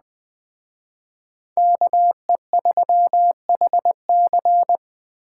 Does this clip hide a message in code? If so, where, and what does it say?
Yes: Morse code from 1.6 to 4.8 s, heard as a keyed tone, about 20 wpm, 700 Hz. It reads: KE3HC